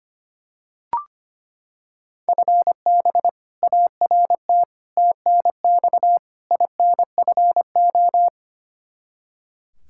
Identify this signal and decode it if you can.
Morse code, heard as a keyed tone, about 25 wpm, 700 Hz. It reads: FB ART TNX INFO
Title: K1USN SST QSO Practice 25wpm